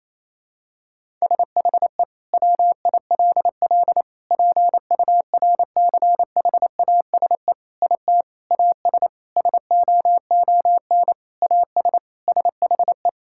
Morse code, heard as a keyed tone, about 27 wpm, 700 Hz. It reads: SHE WILL PURCHASE IT AS SOON AS SHE
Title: Sentences from Top 1000 Words - Review Part 5 - 28wpm